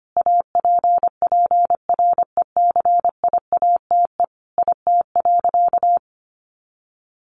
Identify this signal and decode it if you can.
Morse code, heard as a keyed tone, about 25 wpm, 700 Hz. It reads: APPRECIATE IT.
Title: Sentences from Top 1300 Words 25wpm